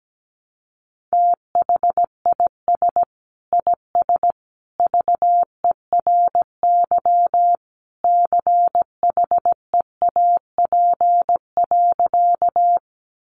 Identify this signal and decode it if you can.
Morse code, heard as a keyed tone, about 17 wpm, 700 Hz. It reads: THIS IS VERY CHEAP.
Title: Sentences from Top 1200 Words 17wpm